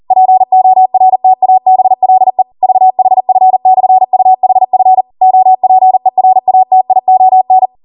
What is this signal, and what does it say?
Morse code, heard as a keyed tone, about 40 wpm, 750 Hz. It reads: PORTABLE VHF/UHF OPERATION